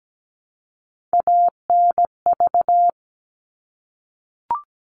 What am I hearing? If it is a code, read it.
Morse code, heard as a keyed tone, about 17 wpm, 700 Hz. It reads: ANV
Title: All Cut Numbers - 3-Digit 17wpm